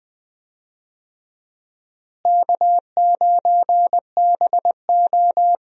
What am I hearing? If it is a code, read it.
Morse code, heard as a keyed tone, about 20 wpm, 700 Hz. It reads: K9BO
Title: USA Call Signs - 4 Characters - Repeated 2x - 20wpm